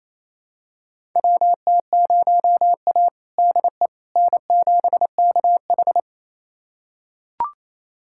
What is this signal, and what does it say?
Morse code, heard as a keyed tone, about 28 wpm, 700 Hz. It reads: WT0A DE N7KH
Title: USA Callsign DE Callsign - 4 Characters - 28wpm